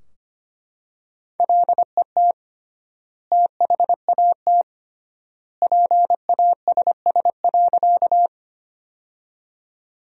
Morse code, heard as a keyed tone, about 25 wpm, 700 Hz. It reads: LET THAT PASS.
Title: 3x Word-Spacing - Sentences from Top 400 Words - 25wpm